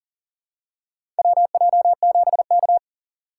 Morse code, heard as a keyed tone, about 40 wpm, 690 Hz. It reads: WJ7K
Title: USA Call Signs - 2x Character Spacing - 40wpm